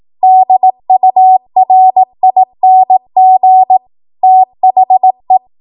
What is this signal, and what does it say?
Morse code, heard as a keyed tone, about 18 wpm, 750 Hz. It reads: DURING THE
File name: Morse / 18WPM